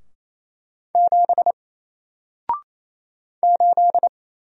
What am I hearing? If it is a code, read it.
Morse code, heard as a keyed tone, about 27 wpm, 700 Hz. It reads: 7 8
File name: Single-Letter-Number-28wpm